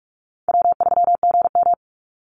Morse code, heard as a keyed tone, about 44 wpm, 700 Hz. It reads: W3GM
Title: USA Call Signs 45wpm